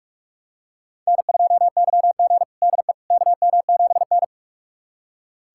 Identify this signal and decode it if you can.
Morse code, heard as a keyed tone, about 47 wpm, 700 Hz. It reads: N2YG DE KM7N
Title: USA Callsign DE Callsign - 4 Characters - 45wpm